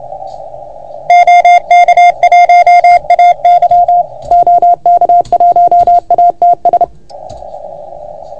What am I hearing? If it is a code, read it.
Morse code, heard as a keyed tone, about 29 wpm, 690 Hz. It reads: OK1AY OK1ATS
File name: Archiv / CUC381 / OK1FGD CUC 381